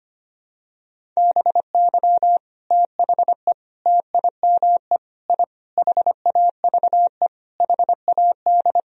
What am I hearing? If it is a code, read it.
Morse code, heard as a keyed tone, about 25 wpm, 700 Hz. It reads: BY THE TIME I HAVE HAD
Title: Sentences from Top 1000 Words - Review Part 1 - 25wpm